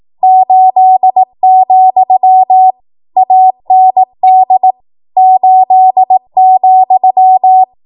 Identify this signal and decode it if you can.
Morse code, heard as a keyed tone, about 18 wpm, 750 Hz. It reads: 8, AND 8,
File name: Morse / 18WPM